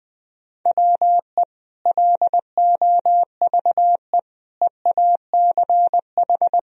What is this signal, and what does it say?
Morse code, heard as a keyed tone, about 20 wpm, 700 Hz. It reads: WE LOVE EACH
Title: Sentences from Top 1000 Words - Review Part 1 - 20wpm